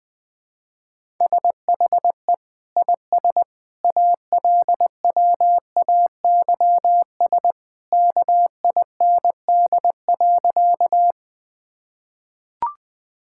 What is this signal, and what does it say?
Morse code, heard as a keyed tone, about 21 wpm, 700 Hz. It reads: SHE IS ALWAYS KIND.
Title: Sentences with 2-Word Phrases - Spoken Beginning - 20wpm